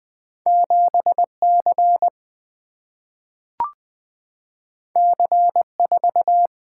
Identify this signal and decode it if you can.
Morse code, heard as a keyed tone, about 20 wpm, 700 Hz. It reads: 7C C4
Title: Single-Letter Single-Number 20wpm